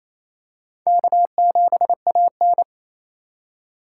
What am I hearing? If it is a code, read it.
Morse code, heard as a keyed tone, about 28 wpm, 700 Hz. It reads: K7AN
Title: USA Call Signs - 2x Character Spacing - 28wpm